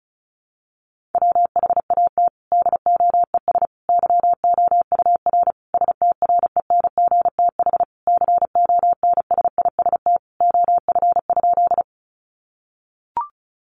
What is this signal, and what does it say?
Morse code, heard as a keyed tone, about 34 wpm, 700 Hz. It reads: WHAT DOES YOUR STRENGTH CONSIST OF?
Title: Sentences from Top 1400 Words 35wpm